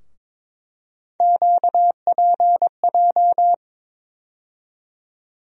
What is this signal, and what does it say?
Morse code, heard as a keyed tone, about 22 wpm, 700 Hz. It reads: QPJ
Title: USA Call Sign Postfixes - 3 Letters 22wpm